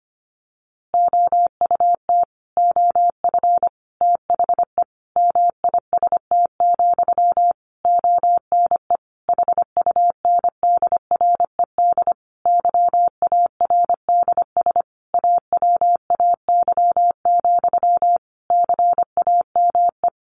Morse code, heard as a keyed tone, about 25 wpm, 700 Hz. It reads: OUT OF THE MIST, ONE HUNDRED YARDS AWAY, CAME
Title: A Sound Of Thunder 25wpm